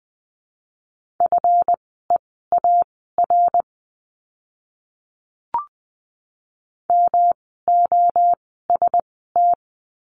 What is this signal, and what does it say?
Morse code, heard as a keyed tone, about 20 wpm, spaced as for 15 wpm, 700 Hz. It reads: FEAR MOST
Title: 139 Lesson 23f - Letter f - Words 20wpm